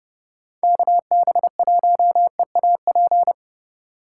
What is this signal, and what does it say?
Morse code, heard as a keyed tone, about 27 wpm, 700 Hz. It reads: KB1EAP